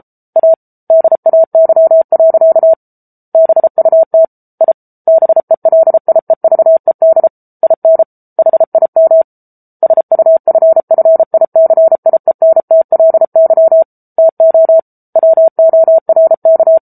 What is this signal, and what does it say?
Morse code, heard as a keyed tone, about 34 wpm, 650 Hz. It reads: A DAY. BUT I BELIEVED IN HIM SUFFICIENTLY TO WORK
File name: Book Two / 07-Chapter 7 - 33 WPM